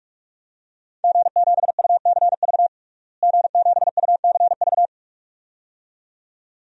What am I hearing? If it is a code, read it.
Morse code, heard as a keyed tone, about 42 wpm, 700 Hz. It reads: G7UCV G7UCV
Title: DXCC-100 Call Signs - 5 Characters - Repeated 2x - 45wpm